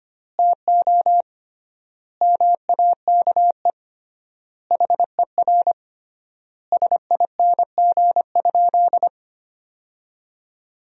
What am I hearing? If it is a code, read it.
Morse code, heard as a keyed tone, about 25 wpm, 700 Hz. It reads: TO MAKE HER SING?
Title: 3x Word-Spacing - Sentences from Top 400 Words - 25wpm